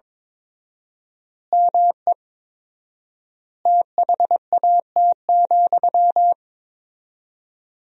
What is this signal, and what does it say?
Morse code, heard as a keyed tone, about 22 wpm, 700 Hz. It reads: ME THAT,
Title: 4x Word-Spacing - Sentences from Top 500 Words - 22wpm